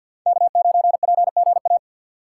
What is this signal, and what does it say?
Morse code, heard as a keyed tone, about 52 wpm, 700 Hz. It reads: K9PGA
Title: USA Call Signs - 5 Characters - 4x Character Spacing - 50wpm